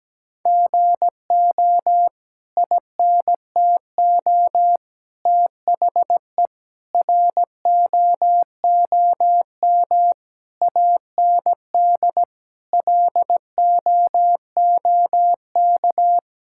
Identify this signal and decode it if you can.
Morse code, heard as a keyed tone, about 17 wpm, 700 Hz. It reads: GO INTO THE ROOM AND LOOK
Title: Sentences from Top 1000 Words - Review Part 6 - 17wpm